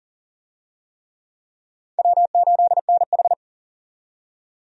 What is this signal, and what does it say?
Morse code, heard as a keyed tone, about 40 wpm, 700 Hz. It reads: W8NH
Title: USA Call Signs - 4 Characters - 40wpm